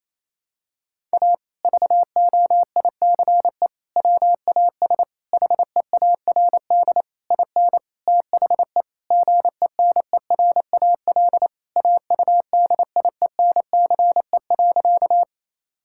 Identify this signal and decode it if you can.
Morse code, heard as a keyed tone, about 28 wpm, 700 Hz. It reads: A VOICE WAS HEARD IN THE GENERAL AUDIENCE.
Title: Sentences from Top 1000 Words - Review Part 5 - 28wpm